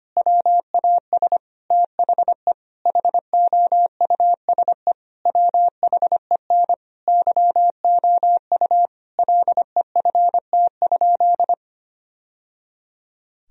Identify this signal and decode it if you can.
Morse code, heard as a keyed tone, about 25 wpm, 700 Hz. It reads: WAS THE HOUSE WHEN YOU LEFT?
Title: Sentences from Top 1000 Words - Review Part 2 - 25wpm